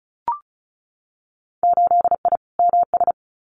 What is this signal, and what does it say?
Morse code, heard as a keyed tone, about 34 wpm, 700 Hz. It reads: 8I MS
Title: Winter Field Day Exchanges Encore 35wpm